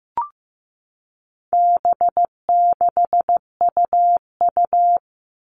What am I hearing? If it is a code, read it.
Morse code, heard as a keyed tone, about 15 wpm, 700 Hz. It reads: B6UU
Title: All Cut Numbers - 4-Digit 15wpm